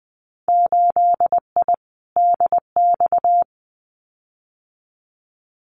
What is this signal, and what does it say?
Morse code, heard as a keyed tone, about 20 wpm, 700 Hz. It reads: 8I DX
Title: Winter Field Day Exchanges Encore 20wpm